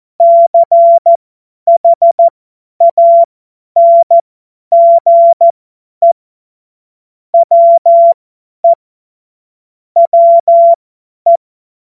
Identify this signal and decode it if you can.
Morse code, heard as a keyed tone, about 14 wpm, spaced as for 10 wpm, 680 Hz. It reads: CHANGE WE WE
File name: Morsekurs / Witze / Witz 14. Dezember 2021, 50 F70